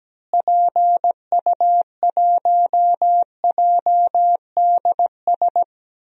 Morse code, heard as a keyed tone, about 17 wpm, 700 Hz. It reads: PU1JDS
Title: DXCC-100 Call Signs - 6 Characters - Repeated 2x - 17wpm